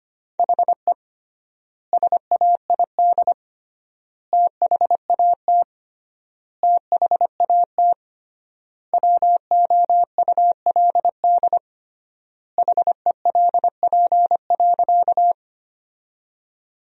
Morse code, heard as a keyed tone, about 25 wpm, 700 Hz. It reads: HE SAID THAT THAT WOULD HELP.
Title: 3x Word-Spacing - Sentences from Top 200 Words - 25wpm